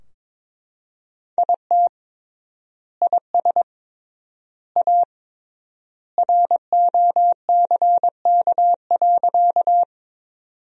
Morse code, heard as a keyed tone, about 22 wpm, 700 Hz. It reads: IT IS A ROCK.
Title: 3x Word-Spacing - Sentences from Top 400 Words - 22wpm